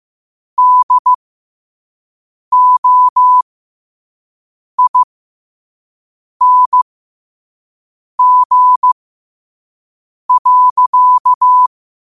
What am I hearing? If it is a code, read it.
Morse code, heard as a keyed tone, about 15 wpm, spaced as for 5 wpm, 1000 Hz. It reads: DOING.
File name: qotdfiles / 5wpm